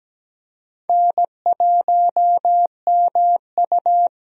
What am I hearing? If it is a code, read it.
Morse code, heard as a keyed tone, about 17 wpm, 700 Hz. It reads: N1MU
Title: USA Call Signs - 4 Characters - Repeated 2x - 17wpm